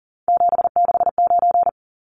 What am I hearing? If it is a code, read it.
Morse code, heard as a keyed tone, about 40 wpm, 700 Hz. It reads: 769
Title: Numbers - 3-Digit 40wpm